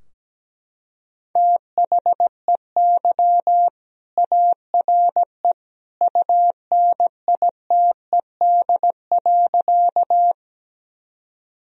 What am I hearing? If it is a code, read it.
Morse code, heard as a keyed tone, about 17 wpm, 700 Hz. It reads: THEY ARE UNITED.
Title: Sentences from Top 1000 Words - Review Part 4 - 17wpm